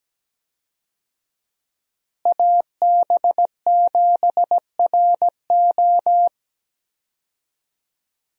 Morse code, heard as a keyed tone, about 17 wpm, 700 Hz. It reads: AB7RO
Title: USA Call Signs - 5 Characters - Repeated 3x - 17wpm